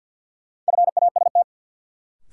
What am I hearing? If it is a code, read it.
Morse code, heard as a keyed tone, about 53 wpm, 700 Hz. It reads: FAST
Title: Sentences from Top 400 Words - New Word Drill - 50wpm